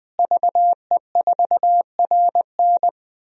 Morse code, heard as a keyed tone, about 20 wpm, 700 Hz. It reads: VE4RN
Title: DXCC-100 Call Signs - 5 Characters - Repeated 3x - 20wpm